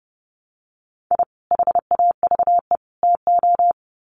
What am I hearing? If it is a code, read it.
Morse code, heard as a keyed tone, about 30 wpm, 700 Hz. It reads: I HAVE TO
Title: Sets of 3 Words 30wpm